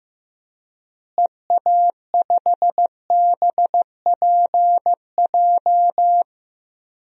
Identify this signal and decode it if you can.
Morse code, heard as a keyed tone, about 15 wpm, 700 Hz. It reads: EA5BPJ
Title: DXCC-100 Call Signs - 6 Characters - Repeated 3x - 15wpm